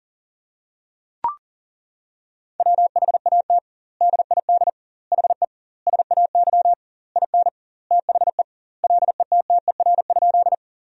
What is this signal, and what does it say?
Morse code, heard as a keyed tone, about 40 wpm, 700 Hz. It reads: WHAT DID HE SAY IN THE LETTER?
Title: 2x Word-Spacing - Sentences from Top 300 Words - 40wpm